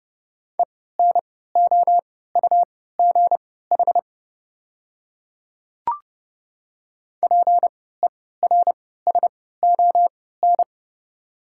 Morse code, heard as a keyed tone, about 31 wpm, spaced as for 17 wpm, 700 Hz. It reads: ENOUGH PERSON
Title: 3x Character Spacing - Top 500 Words - 30wpm